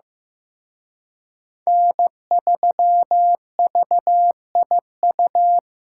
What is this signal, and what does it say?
Morse code, heard as a keyed tone, about 18 wpm, 700 Hz. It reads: N3VIU
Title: Speed Racing - USA 1x3 Call Signs 15wpm